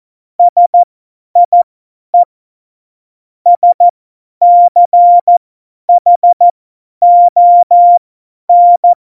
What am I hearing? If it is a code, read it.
Morse code, heard as a keyed tone, about 14 wpm, spaced as for 10 wpm, 700 Hz. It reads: SIE SCHON
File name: Morsekurs / Witze / Witz 15. August 17, Tempo 50 F70